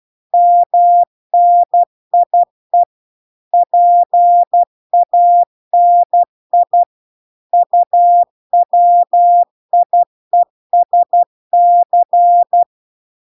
Morse code, heard as a keyed tone, about 12 wpm, 700 Hz. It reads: MNIE PANI UWIESC